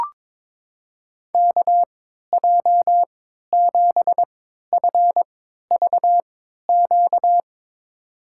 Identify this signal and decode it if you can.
Morse code, heard as a keyed tone, about 22 wpm, spaced as for 13 wpm, 700 Hz. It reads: KJ7FVQ
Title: USA Call Signs - 3x Character Spacing - 22wpm